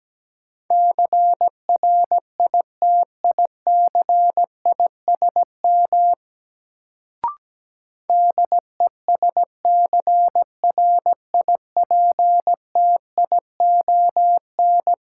Morse code, heard as a keyed tone, about 17 wpm, 700 Hz. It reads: CRITICISM DESCRIPTION
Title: Sentences from Top 1300 Words - New Word Drill Alt - 17wpm